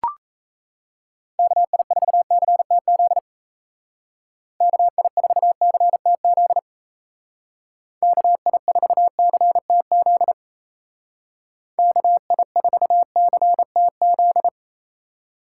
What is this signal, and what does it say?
Morse code, heard as a keyed tone, about 34 wpm, 700 Hz. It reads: KI4CTZ KI4CTZ KI4CTZ KI4CTZ
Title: Speed Racing - USA 2x3 Call Signs 28wpm